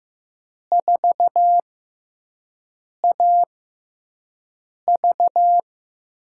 Morse code, heard as a keyed tone, about 15 wpm, spaced as for 5 wpm, 700 Hz. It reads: 4AV
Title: USA Call Signs - 4 Characters - 6x Character Spacing - 15wpm